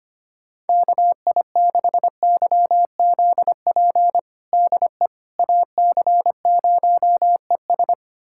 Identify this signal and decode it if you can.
Morse code, heard as a keyed tone, about 25 wpm, 700 Hz. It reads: KI6YZP DE AC0ES
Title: USA Callsign DE Callsign 25wpm